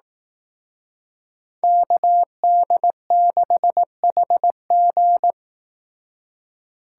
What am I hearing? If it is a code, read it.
Morse code, heard as a keyed tone, about 18 wpm, 700 Hz. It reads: KD6HG
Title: Speed Racing - USA 2x2 Call Signs 15wpm